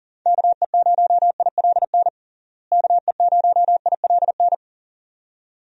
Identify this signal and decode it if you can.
Morse code, heard as a keyed tone, about 40 wpm, 700 Hz. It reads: KE0ILN KE0ILN
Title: USA Call Signs - 6 Characters - Repeated 2x - 40wpm